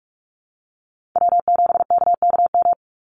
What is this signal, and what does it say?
Morse code, heard as a keyed tone, about 44 wpm, 700 Hz. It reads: W7KKM
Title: NA SOTA Chaser Callsigns 45wpm